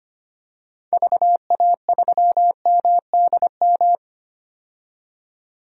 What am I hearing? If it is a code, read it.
Morse code, heard as a keyed tone, about 25 wpm, 700 Hz. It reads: VA3MDM
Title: DXCC-100 Call Signs - 6 Characters - 25wpm